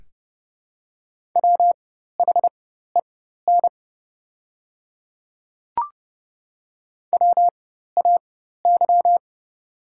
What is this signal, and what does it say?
Morse code, heard as a keyed tone, about 30 wpm, spaced as for 14 wpm, 700 Hz. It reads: WHEN WAY
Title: 4x Character Spacing - Top 100 Words - 30wpm